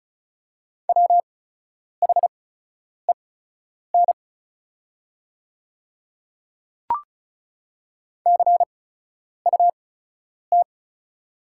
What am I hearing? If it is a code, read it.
Morse code, heard as a keyed tone, about 36 wpm, spaced as for 10 wpm, 700 Hz. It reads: WHEN CUT
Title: 8x Character Spacing - Top 300 Words - 35wpm